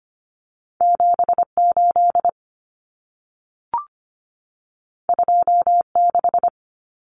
Morse code, heard as a keyed tone, about 25 wpm, 700 Hz. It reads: 78 26
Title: Common Cut Numbers - 2-Digit 25wpm